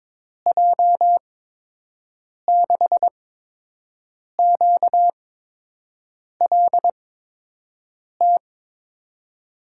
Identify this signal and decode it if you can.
Morse code, heard as a keyed tone, about 22 wpm, spaced as for 6 wpm, 700 Hz. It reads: J6QLT
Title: USA Call Signs - 6 Characters - 8x Character Spacing - 22wpm